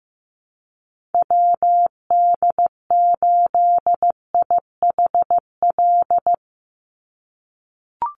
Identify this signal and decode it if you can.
Morse code, heard as a keyed tone, about 15 wpm, 700 Hz. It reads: WD8IHL
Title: USA Call Signs 15wpm